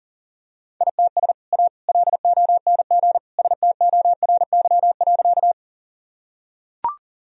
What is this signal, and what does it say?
Morse code, heard as a keyed tone, about 40 wpm, 700 Hz. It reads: ITS A LONG STORY.
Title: Sentences from Top 1000 Words - Review Part 1 - 40wpm